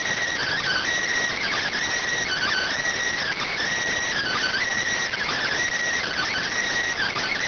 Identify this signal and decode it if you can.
Emergency Alert System (SAME) tones (header bursts), heard as harsh buzzing bursts. It is heard from 0.4 to 6.5 s.